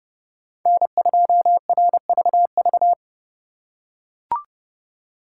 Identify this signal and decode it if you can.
Morse code, heard as a keyed tone, about 28 wpm, 700 Hz. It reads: N2RVV